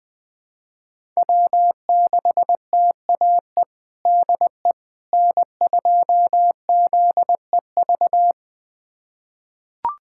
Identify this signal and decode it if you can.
Morse code, heard as a keyed tone, about 20 wpm, 700 Hz. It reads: W6TAE DE N2ZEV
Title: USA Callsign DE Callsign - 5 Characters - 20wpm